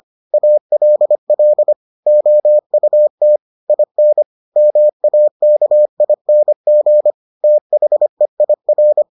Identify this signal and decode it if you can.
Morse code, heard as a keyed tone, about 25 wpm, 600 Hz. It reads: ALL OUT IN MAKING THEIR